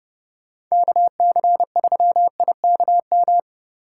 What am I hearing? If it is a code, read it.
Morse code, heard as a keyed tone, about 36 wpm, 700 Hz. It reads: KC3IKM
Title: Speed Racing - USA Call Signs 30wpm